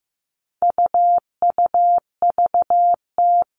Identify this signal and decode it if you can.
Morse code, heard as a keyed tone, about 15 wpm, 700 Hz. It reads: UUVT
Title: All Cut Numbers - 4-Digit 15wpm